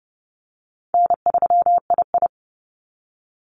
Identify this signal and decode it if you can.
Morse code, heard as a keyed tone, about 30 wpm, 700 Hz. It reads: N3II
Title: NA SOTA Activator Callsigns 30wpm